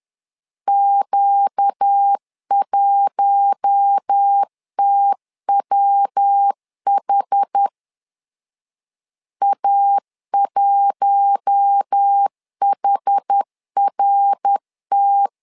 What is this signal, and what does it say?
Morse code, heard as a keyed tone, about 10 wpm, 800 Hz. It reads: Q1TWH A1HRT
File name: LEZIONI / 4 Q1 10wpm 2448